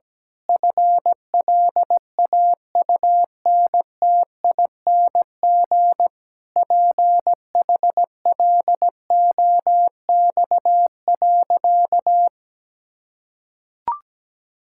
Morse code, heard as a keyed tone, about 17 wpm, 700 Hz. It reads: FLAUNTING PHLOX.
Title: Pangram Challenge v2 - 17wpm